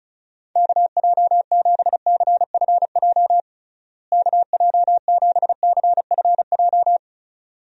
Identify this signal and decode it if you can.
Morse code, heard as a keyed tone, about 34 wpm, 700 Hz. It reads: KJ7CFJ KJ7CFJ
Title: USA Call Signs - 6 Characters - Repeated 2x - 35wpm